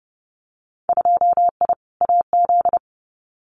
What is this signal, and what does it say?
Morse code, heard as a keyed tone, about 30 wpm, 700 Hz. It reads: 2I AZ
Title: Winter Field Day Exchanges 30wpm